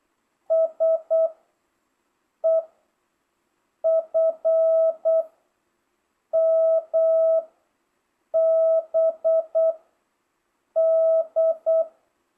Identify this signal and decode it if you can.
Morse code, heard as a keyed tone, about 8 wpm, 650 Hz. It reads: S E F M B D